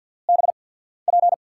Morse code, heard as a keyed tone, about 53 wpm, spaced as for 14 wpm, 700 Hz. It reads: BP
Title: USA Call Signs - 6 Characters - 8x Character Spacing - 50wpm